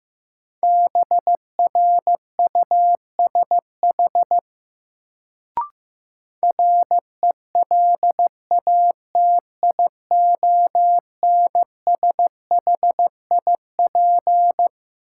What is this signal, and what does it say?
Morse code, heard as a keyed tone, about 15 wpm, 700 Hz. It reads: BRUSH RELATIONSHIP
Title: Sentences from Top 1100 Words - New Word Drill - 15wpm